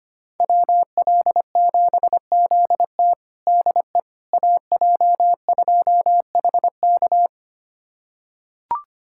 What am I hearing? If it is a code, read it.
Morse code, heard as a keyed tone, about 25 wpm, 700 Hz. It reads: WL7ZT DE AJ2HK
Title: USA Callsign DE Callsign - 5 Characters - 25wpm